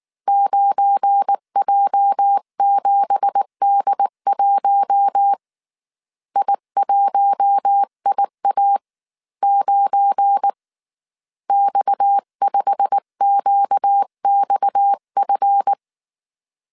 Morse code, heard as a keyed tone, about 19 wpm, 800 Hz. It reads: 9J7D1 I1IA9 X5QXF
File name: CW / DET / 1 9 1500C S B 18wpm 2551